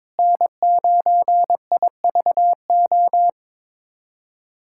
Morse code, heard as a keyed tone, about 22 wpm, 700 Hz. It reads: N9IVO
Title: USA Callsign DE Callsign - 5 Characters - 22wpm